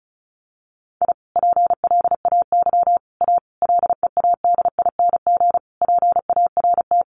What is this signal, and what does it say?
Morse code, heard as a keyed tone, about 34 wpm, 700 Hz. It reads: I PLAY A LEADING PART
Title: Sentences from Top 1100 Words 35wpm